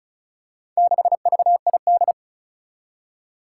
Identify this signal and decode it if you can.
Morse code, heard as a keyed tone, about 36 wpm, 700 Hz. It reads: 6VID
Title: Four Challenging Letter-Number Combinations 35wpm